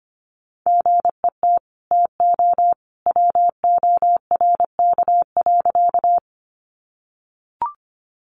Morse code, heard as a keyed tone, about 25 wpm, 700 Hz. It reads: GET TO WORK.
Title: Sentences from Top 200 Words 25wpm